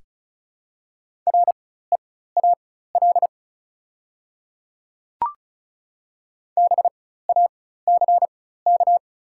Morse code, heard as a keyed tone, about 34 wpm, spaced as for 16 wpm, 690 Hz. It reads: REAL BACK
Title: 4x Character Spacing - Top 500 Words - 35wpm